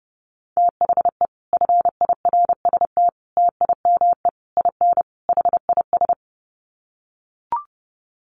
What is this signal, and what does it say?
Morse code, heard as a keyed tone, about 30 wpm, 700 Hz. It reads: THE FIRST TIME IN HIS
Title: Sets of 5 Words Encore2 30wpm